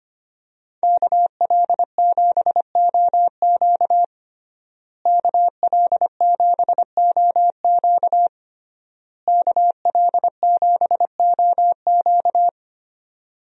Morse code, heard as a keyed tone, about 25 wpm, 700 Hz. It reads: KL7OQ KL7OQ KL7OQ
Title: DXCC-100 Call Signs - 5 Characters - Repeated 3x - 25wpm